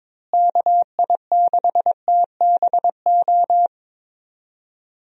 Morse code, heard as a keyed tone, about 22 wpm, 700 Hz. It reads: KI6TBO
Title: USA Call Signs - 6 Characters - Repeated 3x - 22wpm